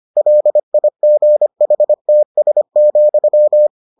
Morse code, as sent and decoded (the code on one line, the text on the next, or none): .-.. .. --. .... - ... --..--
LIGHTS,